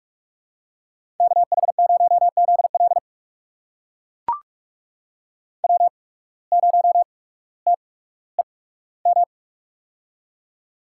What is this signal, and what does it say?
Morse code, heard as a keyed tone, about 44 wpm, spaced as for 19 wpm, 700 Hz. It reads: KH0ZL W0TEM
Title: USA Call Signs - 5 Characters - 8x Character Spacing - 45wpm